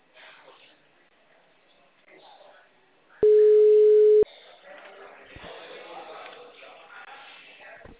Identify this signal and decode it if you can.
Ringback tone (Europe, 425 Hz), with speech.
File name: out- -2005- - - .32227